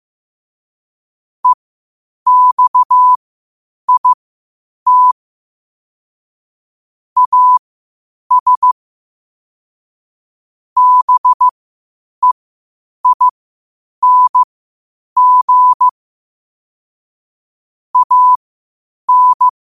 Morse code, heard as a keyed tone, about 15 wpm, spaced as for 8 wpm, 1000 Hz. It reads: EXIT AS BEING AN
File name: qotdfiles / 8wpm